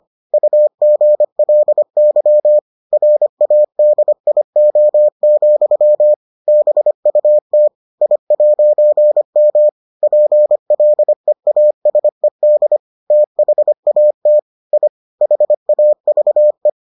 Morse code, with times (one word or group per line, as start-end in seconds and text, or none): 0.3-2.6 UGLY
2.9-6.1 RADIO,
6.5-7.7 BUT
8.0-9.7 I'M
10.0-12.8 PLEASED
13.1-14.4 THAT
14.7-14.9 I
15.2-16.7 HAVE